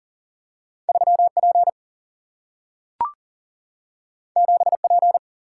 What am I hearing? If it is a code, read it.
Morse code, heard as a keyed tone, about 40 wpm, 700 Hz. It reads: 3P 7P
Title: Single-Letter Single-Number 40wpm